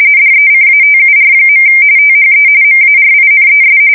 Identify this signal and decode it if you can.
RTTY, about 45.45 baud. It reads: DECATUR. THE MARINES
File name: fsk170 / usnavy1